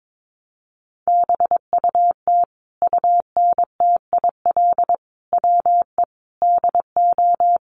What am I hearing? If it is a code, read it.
Morse code, heard as a keyed tone, about 22 wpm, 700 Hz. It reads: BUT UNTIL WE DO
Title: A Sound Of Thunder 22wpm